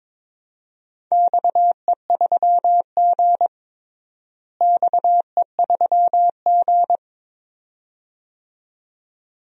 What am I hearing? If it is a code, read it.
Morse code, heard as a keyed tone, about 21 wpm, 700 Hz. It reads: XE3G XE3G
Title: DXCC-100 Call Signs - 4 Characters - Repeated 2x - 22wpm